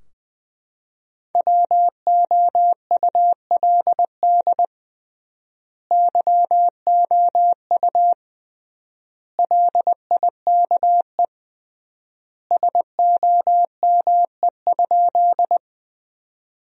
Morse code, heard as a keyed tone, about 20 wpm, 700 Hz. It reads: WOULD YOU LIKE SOME?
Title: 3x Word-Spacing - Sentences from Top 100 Words - 20wpm